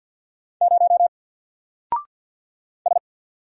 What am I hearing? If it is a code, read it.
Morse code, heard as a keyed tone, about 48 wpm, 700 Hz. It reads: 0 S
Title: Single Character (Letter:Number:Punctuation:Prosign) Alt 50wpm